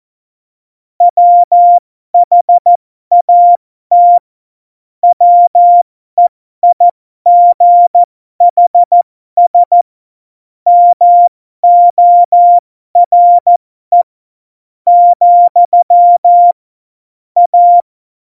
Morse code, heard as a keyed tone, about 14 wpm, spaced as for 11 wpm, 700 Hz. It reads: WHAT WEIGHS MORE, A
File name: Morsekurs / Witze